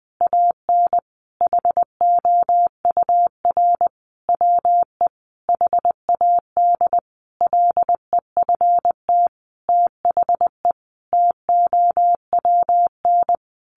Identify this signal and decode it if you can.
Morse code, heard as a keyed tone, about 20 wpm, 700 Hz. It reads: AN HOUR WE HAD LEFT THE TOWN
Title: Sentences from Top 1300 Words 20wpm